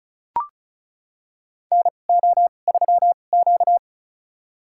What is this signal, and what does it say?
Morse code, heard as a keyed tone, about 34 wpm, spaced as for 25 wpm, 700 Hz. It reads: NO3Q
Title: USA Call Signs - 4 Characters - 2x Character Spacing - 35wpm